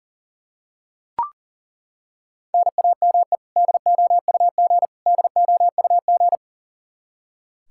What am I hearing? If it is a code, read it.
Morse code, heard as a keyed tone, about 40 wpm, 700 Hz. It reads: NAME DOUG DOUG
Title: Practice QSO - Exchange 1 - Beginning - 40wpm